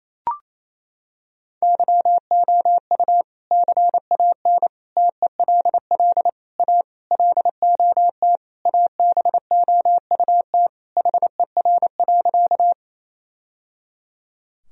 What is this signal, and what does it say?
Morse code, heard as a keyed tone, about 28 wpm, 700 Hz. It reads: YOU CAN TELL A LOT ABOUT HER.
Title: Sentences from Top 1000 Words - Review Part 3 - 28wpm